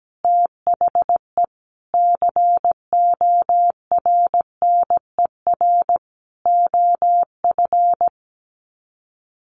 Morse code, heard as a keyed tone, about 17 wpm, 700 Hz. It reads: THE CORNER OF
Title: Sets of 5 Words 17wpm